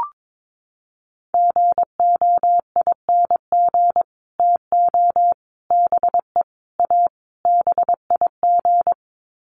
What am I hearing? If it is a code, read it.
Morse code, heard as a keyed tone, about 22 wpm, 700 Hz. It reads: GOING TO BE A BIG
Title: Sets of 5 Words Encore2 22wpm